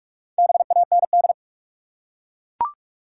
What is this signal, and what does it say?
Morse code, heard as a keyed tone, about 44 wpm, 700 Hz. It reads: BAND
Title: Four Letter Words - 45wpm